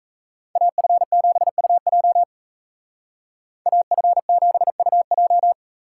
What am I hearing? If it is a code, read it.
Morse code, heard as a keyed tone, about 33 wpm, 700 Hz. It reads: AF7UJ AF7UJ
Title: Speed Racing - USA 2x2 Call Signs 28wpm